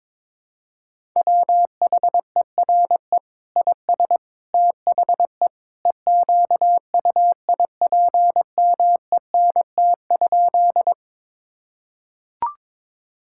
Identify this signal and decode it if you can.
Morse code, heard as a keyed tone, about 21 wpm, 700 Hz. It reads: WHERE IS THE EQUIPMENT?
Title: Sentences from Top 1000 Words - Review Part 5 - 22wpm